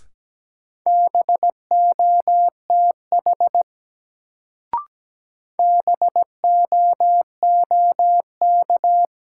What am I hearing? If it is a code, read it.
Morse code, heard as a keyed tone, about 17 wpm, 700 Hz. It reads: BOTH BOOK
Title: Sentences from Top 300 Words - New Word Drill - 17wpm